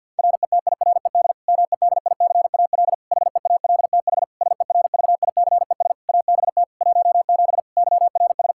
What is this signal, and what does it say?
Morse code, heard as a keyed tone, about 52 wpm, 700 Hz. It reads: RETIRED MEDICAL HEALTH SERVICES ABT 17 YRS
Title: K1USN SST QSO Practice 50wpm